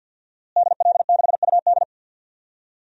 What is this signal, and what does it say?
Morse code, heard as a keyed tone, about 50 wpm, 700 Hz. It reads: DL6UD
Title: DXCC-100 Call Signs - 5 Characters - Repeated 3x - 50wpm